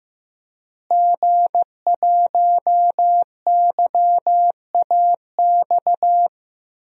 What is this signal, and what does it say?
Morse code, heard as a keyed tone, about 15 wpm, 700 Hz. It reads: G1YAX
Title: DXCC-100 Call Signs - 5 Characters - Repeated 3x - 15wpm